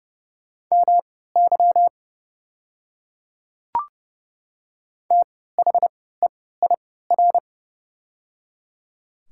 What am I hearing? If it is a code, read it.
Morse code, heard as a keyed tone, about 30 wpm, spaced as for 17 wpm, 700 Hz. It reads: MY THEIR
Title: 3x Character Spacing - Top 100 Words - 30wpm